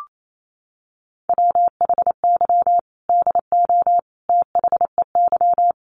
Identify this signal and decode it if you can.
Morse code, heard as a keyed tone, about 28 wpm, 700 Hz. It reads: WHY DO THEY
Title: Sentences from Top 700 Words 28wpm